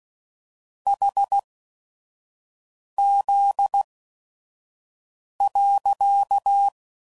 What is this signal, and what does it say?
Morse code, heard as a keyed tone, about 16 wpm, spaced as for 5 wpm, 800 Hz. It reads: HZ.
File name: CW Code Groups @5WPM